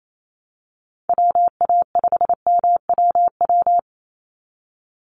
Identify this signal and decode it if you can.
Morse code, heard as a keyed tone, about 28 wpm, 700 Hz. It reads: WA5MWW
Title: USA Call Signs 28wpm